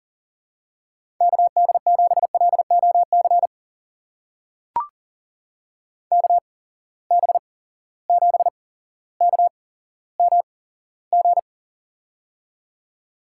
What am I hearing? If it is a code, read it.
Morse code, heard as a keyed tone, about 40 wpm, spaced as for 17 wpm, 700 Hz. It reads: KD7LOC KB7KMG
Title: USA Call Signs - 6 Characters - 8x Character Spacing - 40wpm